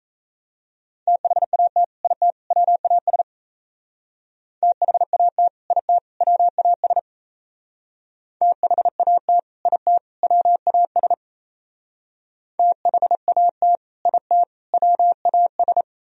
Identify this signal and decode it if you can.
Morse code, heard as a keyed tone, about 37 wpm, 700 Hz. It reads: THAT IT WAS THAT IT WAS THAT IT WAS THAT IT WAS